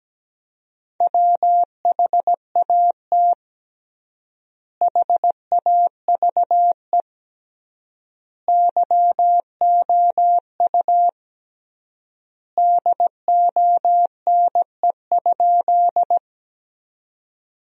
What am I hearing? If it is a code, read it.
Morse code, heard as a keyed tone, about 17 wpm, 700 Hz. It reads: WHAT HAVE YOU DONE?
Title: 3x Word-Spacing - Sentences from Top 500 Words - 17wpm